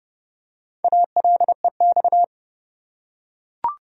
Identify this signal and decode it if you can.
Morse code, heard as a keyed tone, about 30 wpm, 700 Hz. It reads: ALEX
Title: USA Common Names - Max 5 Letters - 30wpm